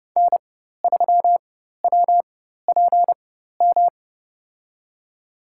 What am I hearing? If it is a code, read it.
Morse code, heard as a keyed tone, about 30 wpm, spaced as for 14 wpm, 700 Hz. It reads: N3WPM
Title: USA Call Signs - 4x Character Spacing - 30wpm